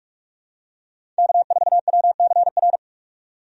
Morse code, heard as a keyed tone, about 46 wpm, 700 Hz. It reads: K4WCR
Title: USA Call Signs - 5 Characters - 45wpm